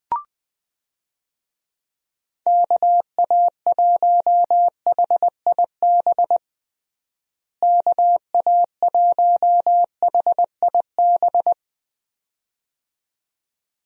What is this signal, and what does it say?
Morse code, heard as a keyed tone, about 20 wpm, 700 Hz. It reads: KA1HIB KA1HIB
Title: USA Call Signs - 6 Characters - Repeated 2x - 20wpm